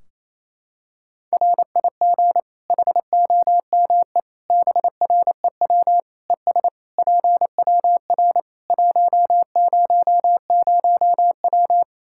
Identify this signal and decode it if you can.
Morse code, heard as a keyed tone, about 28 wpm, 700 Hz. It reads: RIG HOME BREW ES PWR 100W
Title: Practice QSO - Exchange 2 - Info - 28wpm